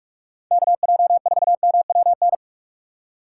Morse code, heard as a keyed tone, about 46 wpm, 700 Hz. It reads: KJ4MWN